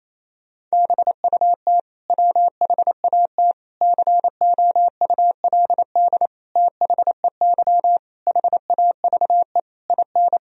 Morse code, heard as a keyed tone, about 28 wpm, 700 Hz. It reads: BUT WHAT COULD THEY HAVE IN
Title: Sentences from Top 1000 Words - Review Part 3 - 28wpm